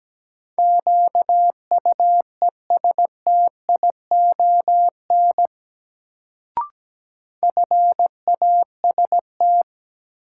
Morse code, heard as a keyed tone, about 17 wpm, 700 Hz. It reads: QUESTION FAST
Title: Sentences from Top 400 Words - New Word Drill - 17wpm